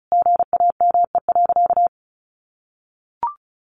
Morse code, heard as a keyed tone, about 34 wpm, 700 Hz. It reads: GAME.
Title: Sentences from Top 900 Words 35wpm